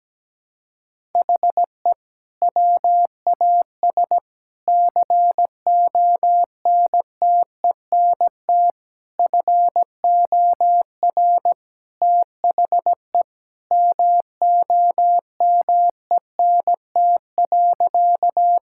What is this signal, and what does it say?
Morse code, heard as a keyed tone, about 17 wpm, 700 Hz. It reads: HE WAS CONTENT FOR THE MOMENT.
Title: Sentences from Top 1000 Words - Review Part 6 - 17wpm